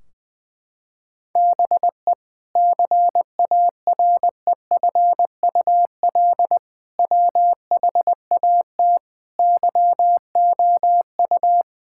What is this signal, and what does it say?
Morse code, heard as a keyed tone, about 21 wpm, 700 Hz. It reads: BE CAREFUL WHAT YOU
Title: Sentences from Top 1600 Words 20wpm